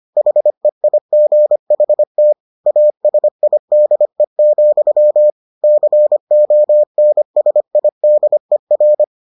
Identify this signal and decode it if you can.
Morse code, heard as a keyed tone, about 25 wpm, 600 Hz. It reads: HEIGHT ASIDE, CONSIDER